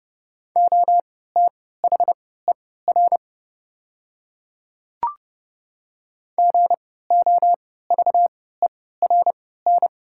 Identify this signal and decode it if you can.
Morse code, heard as a keyed tone, about 30 wpm, spaced as for 17 wpm, 690 Hz. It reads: OTHER GOVERN
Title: 3x Character Spacing - Top 500 Words - 30wpm